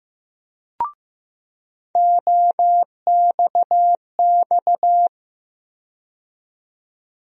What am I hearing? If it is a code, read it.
Morse code, heard as a keyed tone, about 15 wpm, 700 Hz. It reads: OXX
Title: USA Call Sign Postfixes - 3 Letters 15wpm